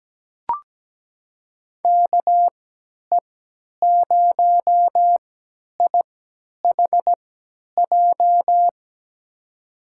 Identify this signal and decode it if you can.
Morse code, heard as a keyed tone, about 17 wpm, spaced as for 10 wpm, 700 Hz. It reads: KE0IHJ
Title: USA Call Signs - 6 Characters - 3x Character Spacing - 17wpm